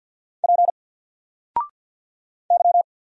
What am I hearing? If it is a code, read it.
Morse code, heard as a keyed tone, about 50 wpm, 700 Hz. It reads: P Y
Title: Single Character (Letter:Number:Punctuation:Prosign) 50wpm